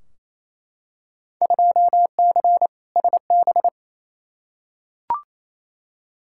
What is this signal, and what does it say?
Morse code, heard as a keyed tone, about 28 wpm, 700 Hz. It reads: 2C SB
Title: Field Day Exchanges Encore - 28wpm